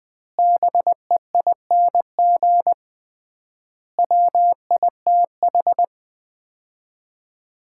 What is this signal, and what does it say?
Morse code, heard as a keyed tone, about 20 wpm, 700 Hz. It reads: BEING WITH
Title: 3x Word-Spacing - Sets of 5 Words - 20wpm